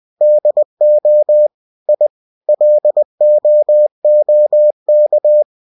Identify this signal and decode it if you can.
Morse code, heard as a keyed tone, about 20 wpm, 600 Hz. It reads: DO I LOOK
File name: cw-books / winnie-the-pooh / winnie-20wpm-page-0006